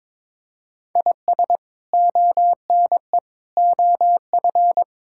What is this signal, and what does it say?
Morse code, heard as a keyed tone, about 22 wpm, 700 Hz. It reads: IS ONE OF
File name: practice / 22wpm / Build A Phrase - 5-Word Phrases - 22wpm